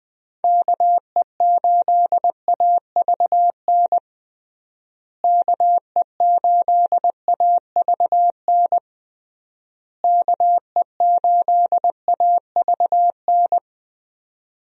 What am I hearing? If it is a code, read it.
Morse code, heard as a keyed tone, about 21 wpm, 700 Hz. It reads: KE8AVN KE8AVN KE8AVN
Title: USA Call Signs - 6 Characters - Repeated 3x - 20wpm